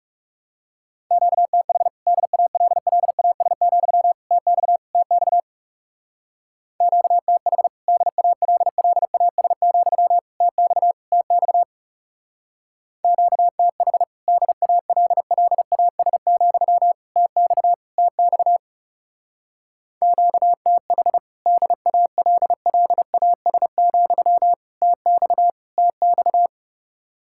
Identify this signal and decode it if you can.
Morse code, heard as a keyed tone, about 36 wpm, 700 Hz. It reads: QTH DALLAS, TX TX QTH DALLAS, TX TX QTH DALLAS, TX TX QTH DALLAS, TX TX
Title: Speed-Racing Common QSO Phrases - 30wpm